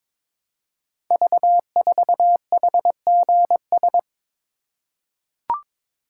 Morse code, heard as a keyed tone, about 22 wpm, 700 Hz. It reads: V4HGS
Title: Code Groups - 5 Letter:Numbers - 22wpm